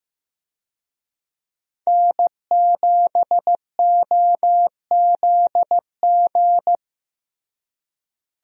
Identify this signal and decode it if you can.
Morse code, heard as a keyed tone, about 15 wpm, 700 Hz. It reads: N7OZG
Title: USA Call Signs - 5 Characters - Repeated 3x - 15wpm